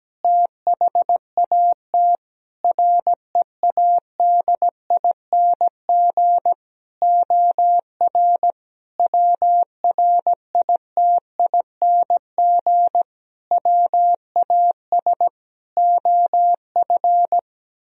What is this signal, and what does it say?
Morse code, heard as a keyed tone, about 17 wpm, 700 Hz. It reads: THAT READING OR WRITING WAS OF
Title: Sentences from Top 1000 Words - Review Part 3 - 17wpm